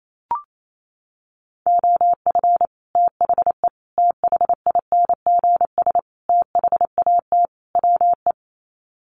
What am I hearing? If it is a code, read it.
Morse code, heard as a keyed tone, about 28 wpm, 700 Hz. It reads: OF THE THINGS THAT WE
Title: Sets of 5 Words Encore 28wpm